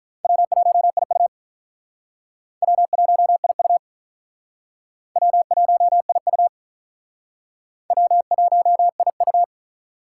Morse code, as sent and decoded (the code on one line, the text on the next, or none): .-- .---- .. ..- / .-- .---- .. ..- / .-- .---- .. ..- / .-- .---- .. ..-
W1IU W1IU W1IU W1IU